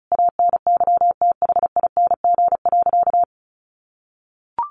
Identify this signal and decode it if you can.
Morse code, heard as a keyed tone, about 34 wpm, 700 Hz. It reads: ANYTHING.
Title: Sentences from Top 1200 Words 35wpm